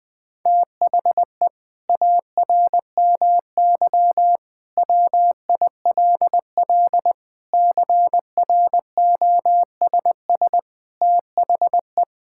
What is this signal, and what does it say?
Morse code, heard as a keyed tone, about 21 wpm, 700 Hz. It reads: THE ARMY WILL CROSS THE
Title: Sentences from Top 1000 Words - Review Part 5 - 20wpm